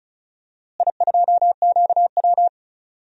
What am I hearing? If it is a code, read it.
Morse code, heard as a keyed tone, about 36 wpm, 700 Hz. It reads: I2QW
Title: DXCC-100 Call Signs - 4 Characters - Repeated 3x - 35wpm